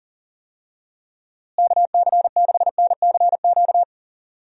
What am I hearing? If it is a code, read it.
Morse code, heard as a keyed tone, about 40 wpm, 700 Hz. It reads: KC6NCQ
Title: USA Call Signs - 6 Characters - Repeated 2x - 40wpm